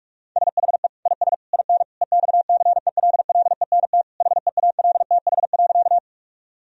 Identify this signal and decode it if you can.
Morse code, heard as a keyed tone, about 46 wpm, 700 Hz. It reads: SHE IS IN EXCELLENT HEALTH.
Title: Sentences from Top 1000 Words - Review Part 3 - 45wpm